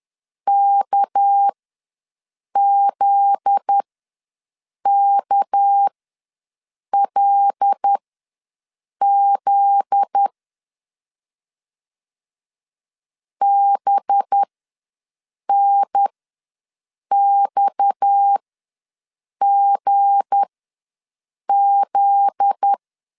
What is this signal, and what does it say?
Morse code, heard as a keyed tone, about 10 wpm, spaced as for 6 wpm, 800 Hz. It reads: KZKLZ BNXGZ
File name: LEZIONI / 7 GF9Z / 7 GF9Z ws 10wpm 3919